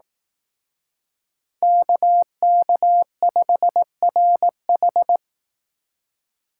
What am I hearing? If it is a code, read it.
Morse code, heard as a keyed tone, about 18 wpm, 700 Hz. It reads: KK5RH
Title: Speed Racing - USA 2x2 Call Signs 15wpm